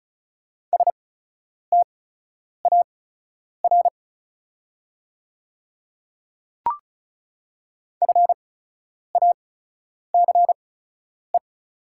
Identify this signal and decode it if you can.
Morse code, heard as a keyed tone, about 36 wpm, spaced as for 10 wpm, 700 Hz. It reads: STAR FACE
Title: 8x Character Spacing - Top 500 Words - 35wpm